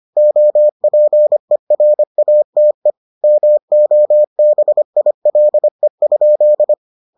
Morse code, heard as a keyed tone, about 25 wpm, 600 Hz. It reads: OPERATE MOBILE?